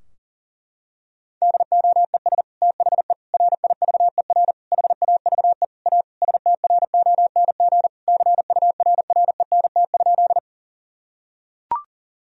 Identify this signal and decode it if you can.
Morse code, heard as a keyed tone, about 40 wpm, 700 Hz. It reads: DOES THE RIVER HAVE A STRONG CURRENT?
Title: Sentences from Top 1000 Words - Review Part 4 - 40wpm